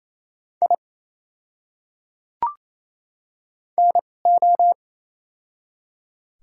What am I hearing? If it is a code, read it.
Morse code, heard as a keyed tone, about 28 wpm, spaced as for 20 wpm, 700 Hz. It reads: I NO
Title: 2x Character Spacing - Top 100 Words - 28wpm